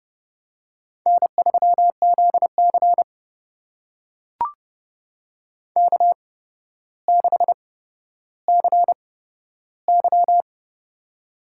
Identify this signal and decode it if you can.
Morse code, heard as a keyed tone, about 31 wpm, spaced as for 13 wpm, 700 Hz. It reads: N3ZC K6CY
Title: USA Call Signs - 4 Characters - 8x Character Spacing - 30wpm